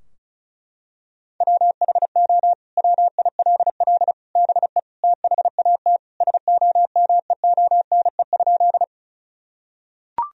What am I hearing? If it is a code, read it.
Morse code, heard as a keyed tone, about 36 wpm, 700 Hz. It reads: WHO WILL BE THAT SOMEONE?
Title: Sentences from Top 1000 Words - Review Part 4 - 35wpm